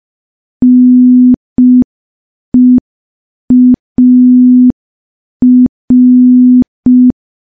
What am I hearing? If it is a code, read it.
Morse code, heard as a keyed tone, about 5 wpm, 250 Hz. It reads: NEAR
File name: morse / morse-training-05WPM